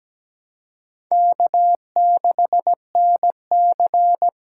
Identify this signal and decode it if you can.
Morse code, heard as a keyed tone, about 20 wpm, 700 Hz. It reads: K6NC
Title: Speed Racing - USA 1x2 Call Signs 17wpm